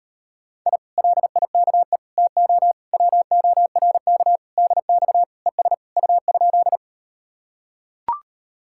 Morse code, heard as a keyed tone, about 38 wpm, 700 Hz. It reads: I LIKE TO WORK DX ES U?